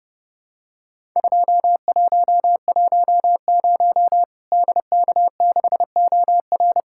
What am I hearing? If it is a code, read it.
Morse code, heard as a keyed tone, about 30 wpm, 700 Hz. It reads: 2110 DK6OR
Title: Worked All Europe DX Contest - QTC Exchange - 30wpm